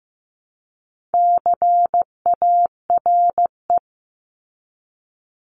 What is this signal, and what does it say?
Morse code, heard as a keyed tone, about 15 wpm, 700 Hz. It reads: CARE
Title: Top 500 Words 15wpm